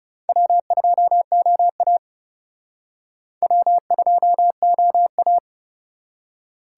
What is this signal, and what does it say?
Morse code, heard as a keyed tone, about 37 wpm, 700 Hz. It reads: W2OA W2OA
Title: Speed Racing - USA 1x2 Call Signs 30wpm